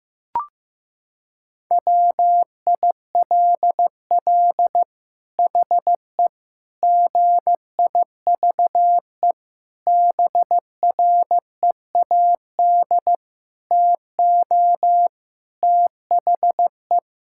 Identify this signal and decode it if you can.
Morse code, heard as a keyed tone, about 15 wpm, 700 Hz. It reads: WILL HE GIVE BREAD TO THE
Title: Sentences from Top 1600 Words 15wpm